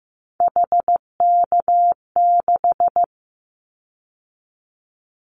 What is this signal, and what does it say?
Morse code, heard as a keyed tone, about 15 wpm, 700 Hz. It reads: HK6
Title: Callsign Prefixes for Top 100 DXCC - 15wpm